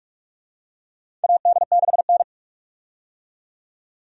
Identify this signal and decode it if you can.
Morse code, heard as a keyed tone, about 46 wpm, 700 Hz. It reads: AD6N
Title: USA Call Signs - 4 Characters - 45wpm